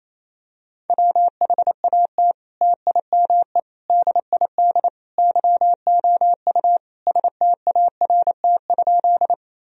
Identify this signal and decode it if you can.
Morse code, heard as a keyed tone, about 28 wpm, 700 Hz. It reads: WHAT TIME DID YOU START?
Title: Sentences from Top 1000 Words - Review Part 1 - 28wpm